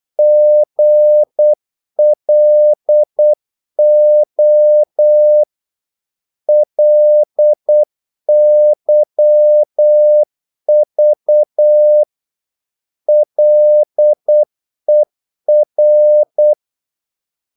Morse code, heard as a keyed tone, about 8 wpm, 600 Hz. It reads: GLO LYV LER
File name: Korte ord DK 8wpm